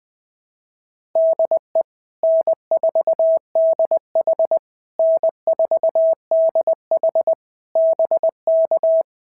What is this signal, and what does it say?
Morse code, heard as a keyed tone, about 20 wpm, 650 Hz. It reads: DE N4DH N4DH BK
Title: POTA QSO Practice - 20wpm